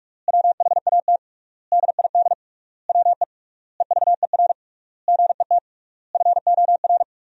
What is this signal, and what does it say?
Morse code, heard as a keyed tone, about 46 wpm, 700 Hz. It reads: WHAT DID WE EVER GET FOR
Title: 3x Word-Spacing - Sentences from Top 400 Words - 45wpm